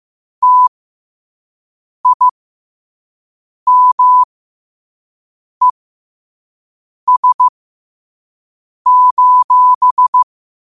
Morse code, heard as a keyed tone, about 15 wpm, spaced as for 5 wpm, 1000 Hz. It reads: TIMES:
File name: qotdfiles / 5wpm